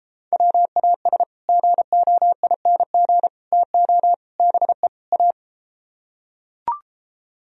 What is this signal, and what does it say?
Morse code, heard as a keyed tone, about 33 wpm, 700 Hz. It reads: WAS GOING TO BE A